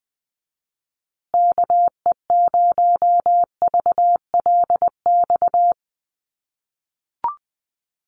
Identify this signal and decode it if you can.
Morse code, heard as a keyed tone, about 20 wpm, 700 Hz. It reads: KE0VLX
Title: USA Call Signs Repeated 3x 20wpm